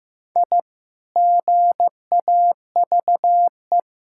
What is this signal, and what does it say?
Morse code, heard as a keyed tone, about 15 wpm, 700 Hz. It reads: I GAVE
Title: Sentences from Top 1000 Words - Review Part 6 - 15wpm